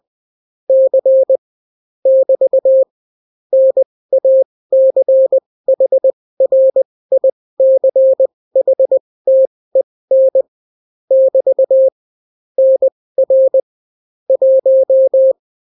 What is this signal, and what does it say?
Morse code, heard as a keyed tone, about 20 wpm, spaced as for 16 wpm, 530 Hz. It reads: C <BT> NACHRICHTEN <BT> NR 1